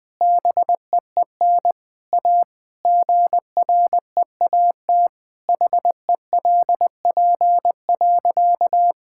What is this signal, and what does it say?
Morse code, heard as a keyed tone, about 21 wpm, 700 Hz. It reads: BEEN A GREAT HELP.
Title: Sentences with 3-Word Phrases - Spoken Beginning - 20wpm